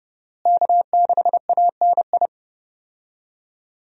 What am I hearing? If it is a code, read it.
Morse code, heard as a keyed tone, about 32 wpm, 700 Hz. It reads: K6ANI
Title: USA Call Signs - 5 Characters - Repeated 3x - 30wpm